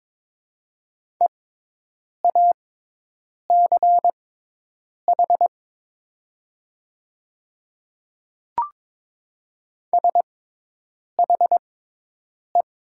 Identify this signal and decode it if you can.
Morse code, heard as a keyed tone, about 22 wpm, spaced as for 8 wpm, 700 Hz. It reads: EACH SHE
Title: 6x Character Spacing - Top 200 Words - 22wpm